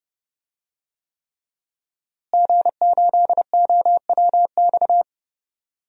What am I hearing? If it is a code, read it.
Morse code, heard as a keyed tone, about 30 wpm, 700 Hz. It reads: G8OWX
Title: DXCC-100 Call Signs - 5 Characters - Repeated 3x - 30wpm